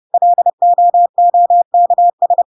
Morse code, heard as a keyed tone, about 30 wpm, 700 Hz. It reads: LOOKS